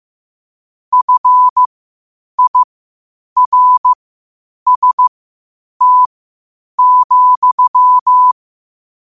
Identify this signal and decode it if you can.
Morse code, heard as a keyed tone, about 15 wpm, spaced as for 8 wpm, 1000 Hz. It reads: FIRST,
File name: qotdfiles / 8wpm